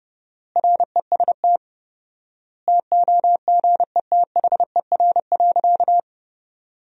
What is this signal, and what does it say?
Morse code, heard as a keyed tone, about 30 wpm, 700 Hz. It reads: REST TOGETHER.
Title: 4x Word-Spacing - Sentences from Top 500 Words - 30wpm